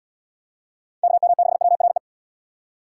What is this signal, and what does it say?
Morse code, heard as a keyed tone, about 70 wpm, 700 Hz. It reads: KM6WRE